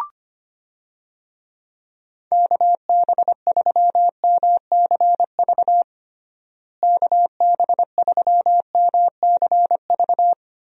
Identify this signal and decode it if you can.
Morse code, heard as a keyed tone, about 25 wpm, 700 Hz. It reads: KB3MCV KB3MCV
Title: USA Call Signs - 6 Characters - Repeated 2x - 25wpm